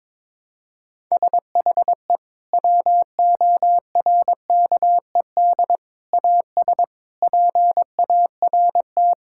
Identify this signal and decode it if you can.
Morse code, heard as a keyed tone, about 22 wpm, 700 Hz. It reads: SHE WORKED AS PART
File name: practice / 22wpm / Sentences with 4-Word Phrases - 22wpm